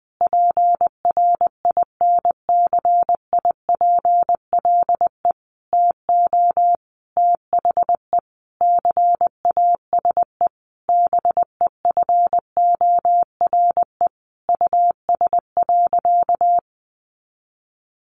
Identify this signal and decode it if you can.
Morse code, heard as a keyed tone, about 20 wpm, 700 Hz. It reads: PRINCIPLE TO THE CASE BEFORE US.
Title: Sentences from Top 1400 Words 20wpm